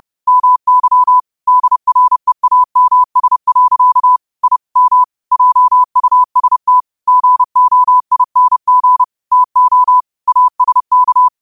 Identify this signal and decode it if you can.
Morse code, heard as a keyed tone, about 30 wpm, 1000 Hz. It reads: MY DREAMS. I M JUST GOING TO ASK
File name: qotdfiles / 30wpm